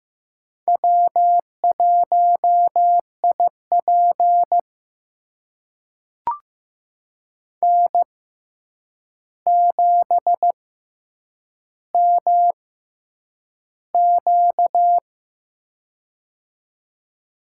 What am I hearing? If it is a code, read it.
Morse code, heard as a keyed tone, about 15 wpm, spaced as for 8 wpm, 700 Hz. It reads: W1IP N7MQ
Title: USA Call Signs - 4 Characters - 6x Character Spacing - 15wpm